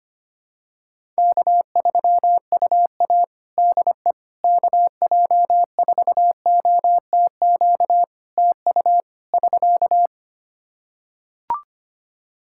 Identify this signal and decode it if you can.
Morse code, heard as a keyed tone, about 25 wpm, 700 Hz. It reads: K3UA DE KJ4OTQ TU <SK>
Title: Practice QSO - Exchange 4 - Ending - 25wpm